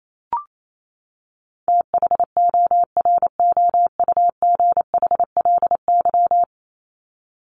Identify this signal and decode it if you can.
Morse code, heard as a keyed tone, about 28 wpm, 700 Hz. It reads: THOROUGHLY
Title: LY Suffix - Top 100 Words with Base Words 28wpm